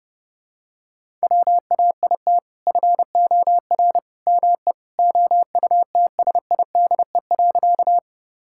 Morse code, heard as a keyed tone, about 30 wpm, 700 Hz. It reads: WAIT FOR ME OUTSIDE.
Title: Sentences from Top 1000 Words - Review Part 6 - 30wpm